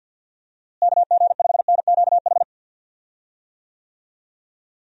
Morse code, heard as a keyed tone, about 52 wpm, 700 Hz. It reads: KG5NQH
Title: USA Call Signs - 6 Characters - Repeated 2x - 50wpm